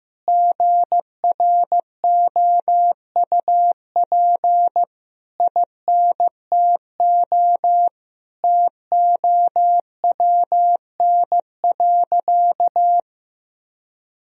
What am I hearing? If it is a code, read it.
Morse code, heard as a keyed tone, about 16 wpm, 700 Hz. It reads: GROUP INTO TOWN.
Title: Sentences from Top 1000 Words - Review Part 1 - 15wpm